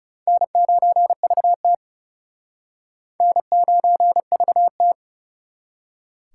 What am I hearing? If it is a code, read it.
Morse code, heard as a keyed tone, about 36 wpm, 700 Hz. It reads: N9VT N9VT
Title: Speed Racing - USA 1x2 Call Signs 30wpm